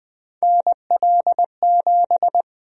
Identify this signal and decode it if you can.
Morse code, heard as a keyed tone, about 20 wpm, 700 Hz. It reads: NL7
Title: USA Call Sign Prefixes - 20wpm